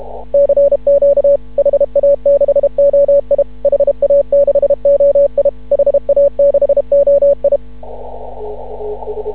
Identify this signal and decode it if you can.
Morse code, heard as a keyed tone, about 31 wpm, 560 Hz. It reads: CQ HA6OI HA6OI HA6OI
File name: CW / ha6oi 2